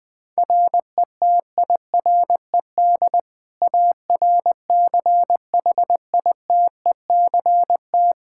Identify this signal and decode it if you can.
Morse code, heard as a keyed tone, about 20 wpm, 700 Hz. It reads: RETIRED ARCHITECT
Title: Practice QSO - Exchange 3 - More Info - 20wpm